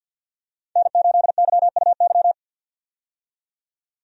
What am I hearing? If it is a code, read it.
Morse code, heard as a keyed tone, about 52 wpm, 700 Hz. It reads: N8YUY
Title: USA Call Signs - 5 Characters - 50wpm